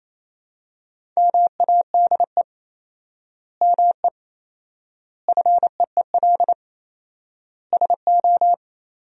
Morse code, heard as a keyed tone, about 28 wpm, 700 Hz. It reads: MADE ME FEEL SO
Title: 4x Word-Spacing - Sentences from Top 400 Words - 28wpm